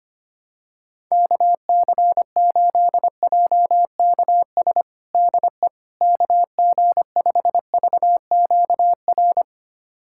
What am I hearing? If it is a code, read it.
Morse code, heard as a keyed tone, about 25 wpm, 700 Hz. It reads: KC8JKS DE KG5VQR
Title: USA Callsign DE Callsign - 6 Characters - 25wpm